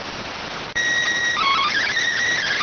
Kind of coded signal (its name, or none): SSTV (slow-scan television)